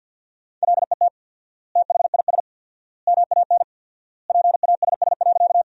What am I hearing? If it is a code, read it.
Morse code, heard as a keyed tone, about 50 wpm, 700 Hz. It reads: LET THIS MAN PASS.
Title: 4x Word-Spacing - Sentences from Top 400 Words - 50wpm